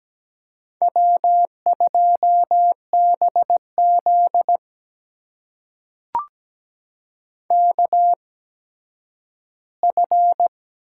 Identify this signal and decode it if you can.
Morse code, heard as a keyed tone, about 17 wpm, spaced as for 10 wpm, 700 Hz. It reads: W2BZ KF
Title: USA Call Signs - 8x Character Spacing - 17wpm